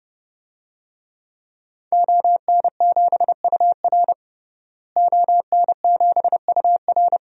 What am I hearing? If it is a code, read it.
Morse code, heard as a keyed tone, about 30 wpm, 700 Hz. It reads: ON7UR ON7UR
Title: DXCC-100 Call Signs - 5 Characters - Repeated 3x - 30wpm